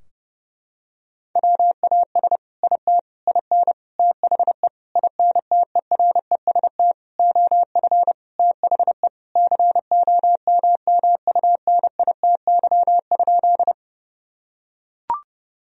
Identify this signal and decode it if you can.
Morse code, heard as a keyed tone, about 30 wpm, 700 Hz. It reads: WAS IT IN THE INTEREST OF THE COMMUNITY?
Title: Sentences from Top 1000 Words - Review Part 4 - 30wpm